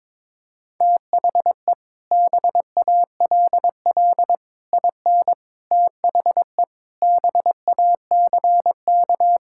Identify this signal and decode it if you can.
Morse code, heard as a keyed tone, about 22 wpm, 700 Hz. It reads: THE BALL IN THE BACK
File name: practice / 22wpm / Sets of 8 Words - 22wpm